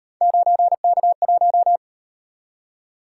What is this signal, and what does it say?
Morse code, heard as a keyed tone, about 38 wpm, 700 Hz. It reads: 9K1